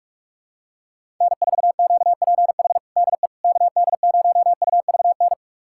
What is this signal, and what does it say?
Morse code, heard as a keyed tone, about 41 wpm, 690 Hz. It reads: N4QPH DE KD0UVN
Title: USA Callsign DE Callsign 45wpm